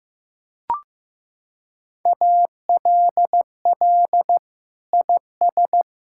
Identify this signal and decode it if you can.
Morse code, heard as a keyed tone, about 15 wpm, 700 Hz. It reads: ALL IS
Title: Sentences from Top 1600 Words 15wpm